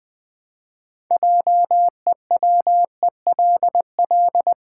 Morse code, heard as a keyed tone, about 20 wpm, 700 Hz. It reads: JEWELL
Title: Build A Word - 3 Letters At A Time - 9-Letter Words - 20wpm